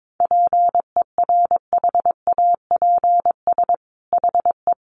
Morse code, heard as a keyed tone, about 22 wpm, 700 Hz. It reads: PERHAPS HE
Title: Sentences from Top 1200 Words 22wpm